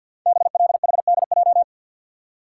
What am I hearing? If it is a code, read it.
Morse code, heard as a keyed tone, about 51 wpm, 700 Hz. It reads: BLHDJ
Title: Code Groups - 5 Letters - 50wpm